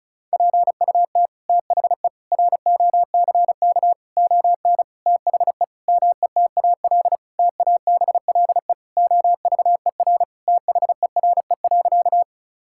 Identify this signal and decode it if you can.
Morse code, heard as a keyed tone, about 36 wpm, 700 Hz. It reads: PUT THE ROCK ON THE METAL TABLE OVER THERE.
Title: Sentences from Top 1000 Words - Review Part 6 - 35wpm